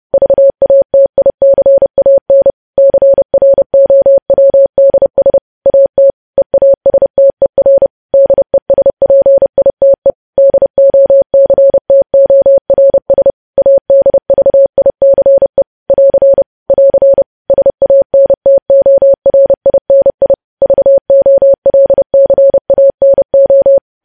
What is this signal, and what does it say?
Morse code, heard as a keyed tone, about 30 wpm, 550 Hz. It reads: VATICAN CROWDS AT EASTER DESPITE DOCTORS ADVICE <AR> <AR> SANTORINI VOLCANO
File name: podcast / News headlines-1x-30x30